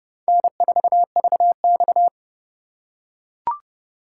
Morse code, heard as a keyed tone, about 29 wpm, 690 Hz. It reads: N4VX
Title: USA Call Signs - 4 Characters - 2x Character Spacing - 30wpm